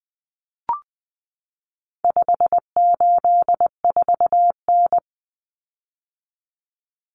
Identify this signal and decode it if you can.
Morse code, heard as a keyed tone, about 20 wpm, 700 Hz. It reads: 584N
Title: Common Cut Numbers - 4-Digit 20wpm